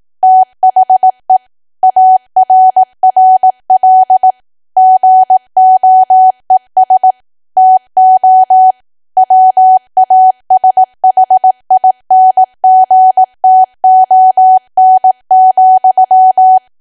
Morse code, heard as a keyed tone, about 18 wpm, 750 Hz. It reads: THE ARRL GOES TO WASHINGTON,
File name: Archive / 18 WPM / 18WPM